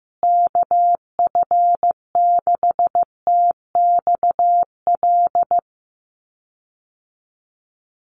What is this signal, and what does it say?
Morse code, heard as a keyed tone, about 15 wpm, 700 Hz. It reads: KF6TXL
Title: USA Call Signs Repeated 3x 15wpm